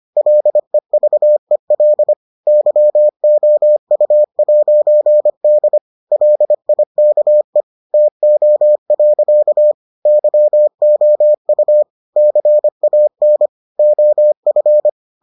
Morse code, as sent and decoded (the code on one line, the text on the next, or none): .-.. ...- . .-.. / -.-- --- ..- .----. -.. / .-.. .. -.- . / - --- .-.-.- / -.-- --- ..- / -.-. .- -. / --- ..-.
LEVEL YOU'D LIKE TO. YOU CAN OF